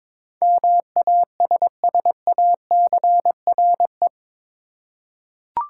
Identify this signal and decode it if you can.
Morse code, heard as a keyed tone, about 22 wpm, 700 Hz. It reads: MASSACRE
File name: practice / 22wpm / Eight Letter Words - 22wpm